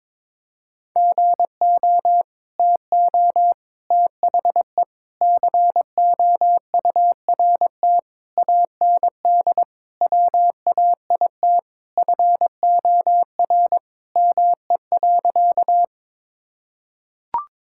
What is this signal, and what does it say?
Morse code, heard as a keyed tone, about 22 wpm, 700 Hz. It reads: GO TO THE COURT AND WAIT FOR ME.
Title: Sentences from Top 1600 Words 22wpm